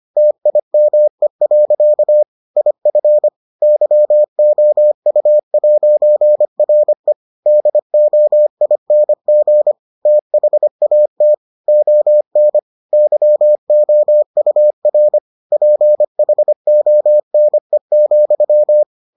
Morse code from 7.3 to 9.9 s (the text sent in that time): DOING